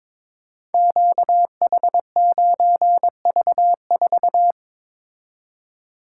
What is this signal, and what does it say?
Morse code, heard as a keyed tone, about 23 wpm, 690 Hz. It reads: QH9V4
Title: Code Groups - 5 Letter:Numbers - 22wpm